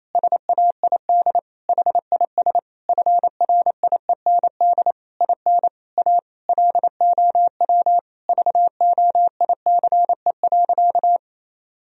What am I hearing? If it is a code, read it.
Morse code, heard as a keyed tone, about 28 wpm, 700 Hz. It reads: SAID HIS FRIEND IN A LOW VOICE.
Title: Sentences from Top 1000 Words - Review Part 1 - 28wpm